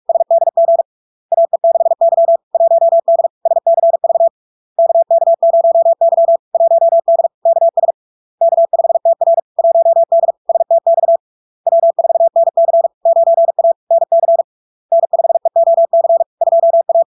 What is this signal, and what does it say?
Morse code, heard as a keyed tone, about 46 wpm, 670 Hz. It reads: SDG AE6Y 1D SCV KK0Y 1D KS K5TR 1D STX W4NC 9A NC N5EOC 2A
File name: LARK-FD-2022-Log-in-CW-45wpm-670-Hz